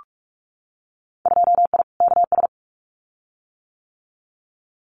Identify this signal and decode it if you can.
Morse code, heard as a keyed tone, about 46 wpm, 700 Hz. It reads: 2I KS
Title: Winter Field Day Exchanges Encore 45wpm